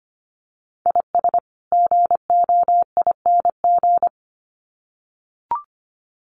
Morse code, heard as a keyed tone, about 25 wpm, 700 Hz. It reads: IS GOING
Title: Sets of 2 Words 25wpm